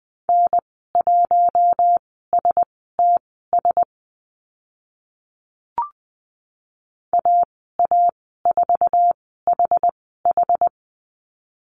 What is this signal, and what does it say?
Morse code, heard as a keyed tone, about 20 wpm, spaced as for 15 wpm, 700 Hz. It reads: N1STS AA4HH
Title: 069 Lesson 12g - Letter d - Call Signs 20wpm